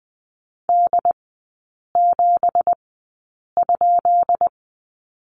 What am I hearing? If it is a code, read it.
Morse code, heard as a keyed tone, about 20 wpm, spaced as for 8 wpm, 700 Hz. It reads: D7?
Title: 184 Lesson 30e - Letter b - Three Characters 20wpm